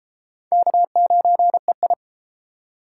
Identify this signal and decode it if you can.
Morse code, heard as a keyed tone, about 32 wpm, 700 Hz. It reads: K9EI